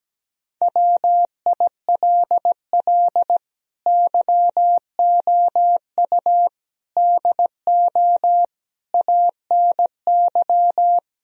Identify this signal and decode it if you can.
Morse code, heard as a keyed tone, about 17 wpm, 700 Hz. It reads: WILL YOU DO ANY
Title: Sentences from Top 1000 Words - Review Part 3 - 17wpm